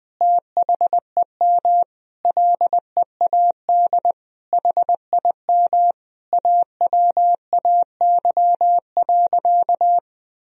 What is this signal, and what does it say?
Morse code, heard as a keyed tone, about 20 wpm, 700 Hz. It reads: THEM LEAD HIM AWAY.
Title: Sentences from Top 1000 Words - Review Part 2 - 20wpm